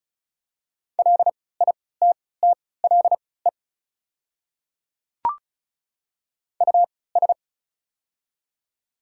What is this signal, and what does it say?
Morse code, heard as a keyed tone, about 34 wpm, spaced as for 20 wpm, 700 Hz. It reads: LITTLE US
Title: 3x Character Spacing - Top 200 Words - 35wpm